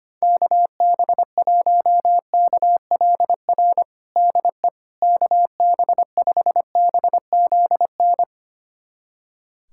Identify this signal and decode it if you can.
Morse code, heard as a keyed tone, about 25 wpm, 700 Hz. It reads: KB1KLR DE KB5BZN
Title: USA Callsign DE Callsign - 6 Characters - 25wpm